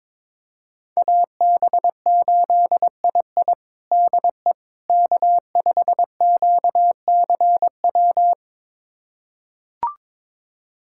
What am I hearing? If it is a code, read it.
Morse code, heard as a keyed tone, about 22 wpm, 700 Hz. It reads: AB8II DE K5QCW
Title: USA Callsign DE Callsign - 5 Characters - 22wpm